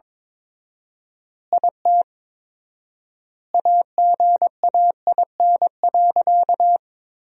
Morse code, heard as a keyed tone, about 22 wpm, 700 Hz. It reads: IT AGAIN.
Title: 4x Word-Spacing - Sentences from Top 300 Words - 22wpm